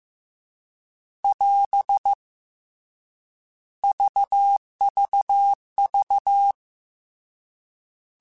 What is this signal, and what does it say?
Morse code, heard as a keyed tone, about 15 wpm, 800 Hz. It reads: <AS> VVV